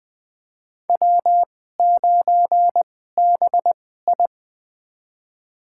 Morse code, heard as a keyed tone, about 20 wpm, spaced as for 15 wpm, 700 Hz. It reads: W9BI
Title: USA Call Signs - 4 Characters - 2x Character Spacing - 20wpm